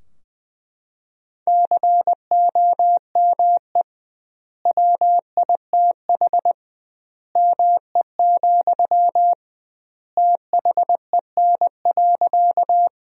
Morse code, heard as a keyed tone, about 20 wpm, 700 Hz. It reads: COME WITH ME, THEN.
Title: 2x Word-Spacing - Sentences from Top 200 Words - 20wpm